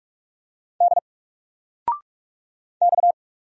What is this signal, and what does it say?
Morse code, heard as a keyed tone, about 44 wpm, 700 Hz. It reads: D X
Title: Single Character (Letter:Number:Punctuation:Prosign) 45wpm